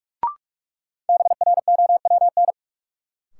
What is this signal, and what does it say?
Morse code, heard as a keyed tone, about 43 wpm, 700 Hz. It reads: BROWN
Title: Sentences from Top 1200 Words - New Word Drill Alt - 45wpm